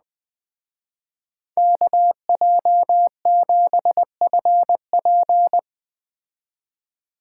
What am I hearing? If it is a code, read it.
Morse code, heard as a keyed tone, about 19 wpm, 700 Hz. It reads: KJ7FP